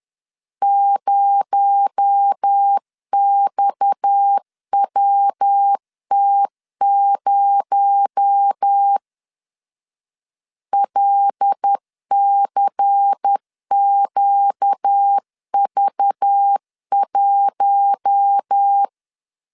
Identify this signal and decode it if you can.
Morse code, heard as a keyed tone, about 10 wpm, 800 Hz. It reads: 0XWT0 LCQV1
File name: LEZIONI / 6 PKX0 / 6 PKX0 10wpm 2707